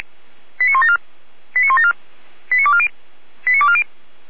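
Five-tone selective call (ZVEI), heard as quick runs of five tones, tones of about 70 ms. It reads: 89174 89174 89240 89240
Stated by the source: ZVEI five-tone selective call received by SDR.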